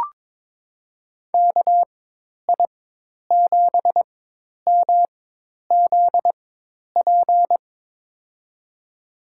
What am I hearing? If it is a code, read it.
Morse code, heard as a keyed tone, about 21 wpm, spaced as for 10 wpm, 700 Hz. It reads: KI7MZP
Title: USA Call Signs - 6 Characters - 4x Character Spacing - 22wpm